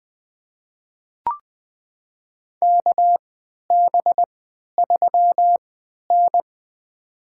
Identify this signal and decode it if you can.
Morse code, heard as a keyed tone, about 20 wpm, spaced as for 11 wpm, 700 Hz. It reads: KB3N
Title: USA Call Signs - 4 Characters - 3x Character Spacing - 20wpm